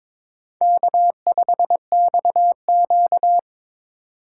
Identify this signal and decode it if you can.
Morse code, heard as a keyed tone, about 22 wpm, 700 Hz. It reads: K5XQ
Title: K1USN SST QSO Practice - 22wpm